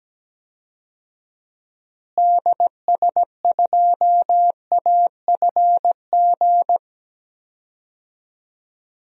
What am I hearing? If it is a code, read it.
Morse code, heard as a keyed tone, about 17 wpm, 700 Hz. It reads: DS2AFG
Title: DXCC-100 Call Signs - 6 Characters - Repeated 2x - 17wpm